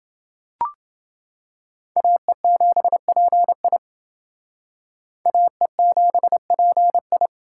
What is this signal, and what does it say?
Morse code, heard as a keyed tone, about 28 wpm, 700 Hz. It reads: AE7PI AE7PI